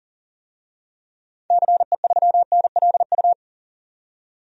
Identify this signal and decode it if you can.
Morse code, heard as a keyed tone, about 40 wpm, 700 Hz. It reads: CE3NLU
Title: DXCC-100 Call Signs - 6 Characters - 40wpm